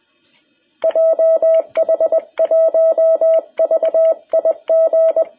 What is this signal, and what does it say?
Morse code, heard as a keyed tone, about 20 wpm, 660 Hz. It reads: JH1VIG
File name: jh8cbh / 2021cw